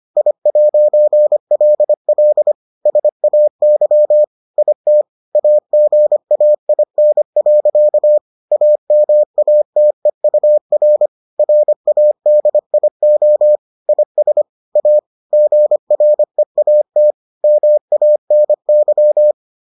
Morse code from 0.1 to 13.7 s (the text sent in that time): I'LL SAY IT AGAIN. AMATEUR RADIO